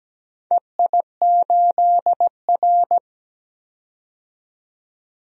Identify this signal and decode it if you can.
Morse code, heard as a keyed tone, about 17 wpm, 700 Hz. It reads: EI8R
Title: DXCC-100 Call Signs - 4 Characters - 17wpm